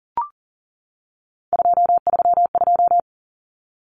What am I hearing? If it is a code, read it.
Morse code, heard as a keyed tone, about 40 wpm, 700 Hz. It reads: 232
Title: Numbers - 3-Digit 40wpm